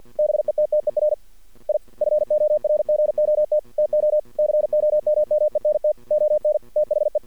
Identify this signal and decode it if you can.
Morse code, heard as a keyed tone, about 53 wpm, 610 Hz. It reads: BETTER I FORGOT TO COMMENT ON THE
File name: archived / CRSnet